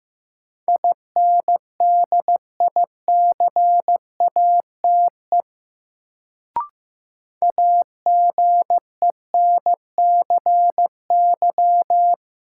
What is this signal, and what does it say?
Morse code, heard as a keyed tone, about 15 wpm, 700 Hz. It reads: INDICATE AGENCY
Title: Sentences from Top 1300 Words - New Word Drill - 15wpm